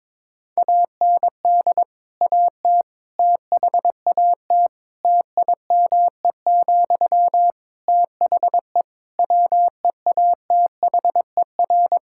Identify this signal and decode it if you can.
Morse code, heard as a keyed tone, about 23 wpm, 700 Hz. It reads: AND AT THAT TIME, THE WEATHER
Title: Sentences from Top 1000 Words - Review Part 5 - 22wpm